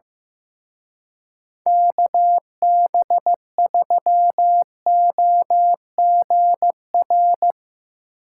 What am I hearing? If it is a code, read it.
Morse code, heard as a keyed tone, about 17 wpm, 700 Hz. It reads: KB3OGR
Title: Speed Racing - USA 2x3 Call Signs 15wpm